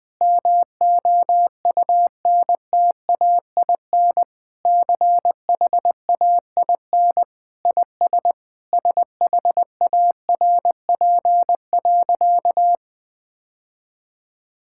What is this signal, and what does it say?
Morse code, heard as a keyed tone, about 20 wpm, 700 Hz. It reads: MOUNTAIN CHAIN IS SHARP.
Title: Sentences from Top 1600 Words 20wpm